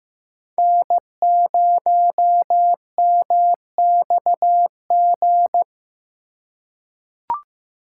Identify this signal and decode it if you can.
Morse code, heard as a keyed tone, about 15 wpm, 700 Hz. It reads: N0MXG
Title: USA Call Signs - 5 Characters - 4x Character Spacing - 15wpm